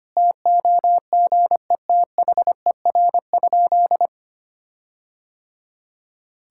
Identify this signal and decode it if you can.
Morse code, heard as a keyed tone, about 25 wpm, 700 Hz. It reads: TOGETHER?
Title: Sentences from Top 1600 Words 25wpm